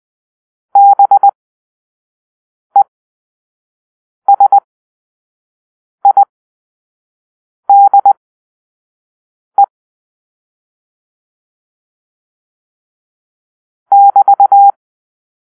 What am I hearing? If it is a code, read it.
Morse code, heard as a keyed tone, about 20 wpm, spaced as for 5 wpm, 800 Hz. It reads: BESIDE <BT>
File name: arrl / Grp-4 Drill-1